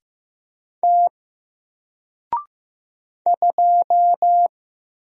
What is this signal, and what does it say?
Morse code, heard as a keyed tone, about 15 wpm, 700 Hz. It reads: T 2
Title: Single Character (Letter:Number:Punctuation:Prosign) Alt 15wpm